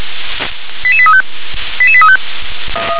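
Five-tone selective call (ZVEI), heard as quick runs of five tones, tones of about 70 ms. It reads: 88835 88835
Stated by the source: ZVEI five-tone selective call received by SDR.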